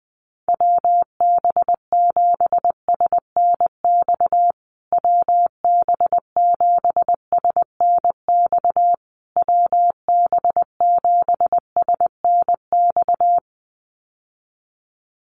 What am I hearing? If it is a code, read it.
Morse code, heard as a keyed tone, about 20 wpm, 700 Hz. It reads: WB7SNX WB7SNX WB7SNX
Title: USA Call Signs Repeated 3x 20wpm